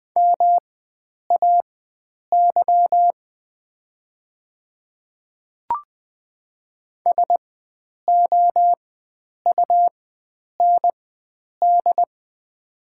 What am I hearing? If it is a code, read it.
Morse code, heard as a keyed tone, about 20 wpm, spaced as for 9 wpm, 700 Hz. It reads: MAY SOUND
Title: 4x Character Spacing - Top 100 Words - 20wpm